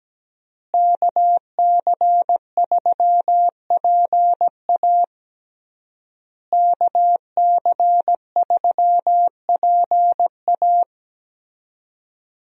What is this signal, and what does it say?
Morse code, heard as a keyed tone, about 17 wpm, 700 Hz. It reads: KC3PA KC3PA
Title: USA Call Signs - 5 Characters - Repeated 3x - 17wpm